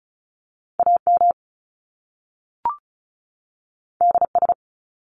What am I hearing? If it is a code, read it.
Morse code, heard as a keyed tone, about 36 wpm, 700 Hz. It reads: AM DS
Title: Top Two Letter Combinations in Words 35wpm